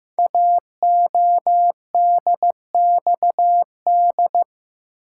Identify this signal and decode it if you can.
Morse code, heard as a keyed tone, about 15 wpm, 700 Hz. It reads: AODXD
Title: Code Groups - 5 Letters - 15wpm